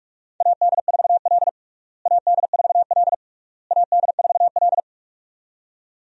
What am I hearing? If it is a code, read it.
Morse code, heard as a keyed tone, about 41 wpm, 700 Hz. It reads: AD4L AD4L AD4L
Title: USA Call Signs - 4 Characters - Repeated 3x - 45wpm